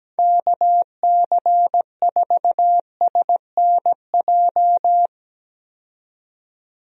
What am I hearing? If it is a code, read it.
Morse code, heard as a keyed tone, about 17 wpm, 700 Hz. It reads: KC4SNJ
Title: USA Call Signs - 6 Characters - 17wpm